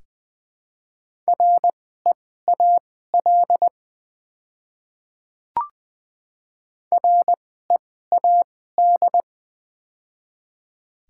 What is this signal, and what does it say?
Morse code, heard as a keyed tone, about 20 wpm, spaced as for 14 wpm, 700 Hz. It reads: REAL READ
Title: 2x Character Spacing - Top 300 Words - 20wpm